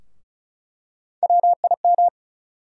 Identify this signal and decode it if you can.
Morse code, heard as a keyed tone, about 34 wpm, 700 Hz. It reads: WIM
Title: USA Call Sign Postfixes - 3 Letters 35wpm